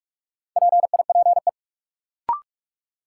Morse code, heard as a keyed tone, about 41 wpm, 700 Hz. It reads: PIPE
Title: Sentences from Top 1500 Words - New Word Drill Alt - 45wpm